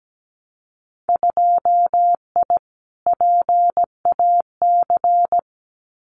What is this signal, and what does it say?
Morse code, heard as a keyed tone, about 17 wpm, 700 Hz. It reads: 2I PAC
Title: Winter Field Day Exchanges Encore 17wpm